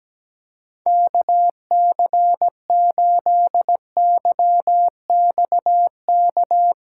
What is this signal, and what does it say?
Morse code, heard as a keyed tone, about 17 wpm, 700 Hz. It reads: KC8YXK
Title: USA Callsign DE Callsign - 6 Characters - 17wpm